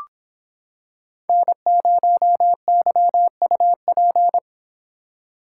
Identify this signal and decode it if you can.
Morse code, heard as a keyed tone, about 25 wpm, 700 Hz. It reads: N0YUP